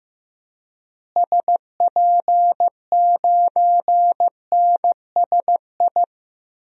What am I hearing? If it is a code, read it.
Morse code, heard as a keyed tone, about 15 wpm, 700 Hz. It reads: SP9NSI
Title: DXCC-100 Call Signs - 6 Characters - Repeated 2x - 15wpm